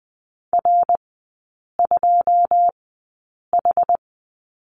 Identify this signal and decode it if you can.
Morse code, heard as a keyed tone, about 20 wpm, spaced as for 8 wpm, 700 Hz. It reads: R2H
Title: 081 Lesson 14e - Number 2 - Three Characters 20wpm